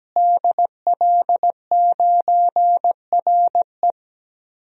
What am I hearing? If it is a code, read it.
Morse code, heard as a keyed tone, about 17 wpm, 700 Hz. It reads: DL9RE
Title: DXCC-100 Call Signs - 5 Characters - Repeated 2x - 17wpm